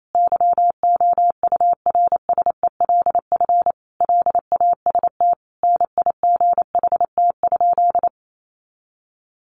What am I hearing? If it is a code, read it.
Morse code, heard as a keyed tone, about 28 wpm, 700 Hz. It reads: YOURSELF LAST NIGHT?
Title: Sentences from Top 600 Words 28wpm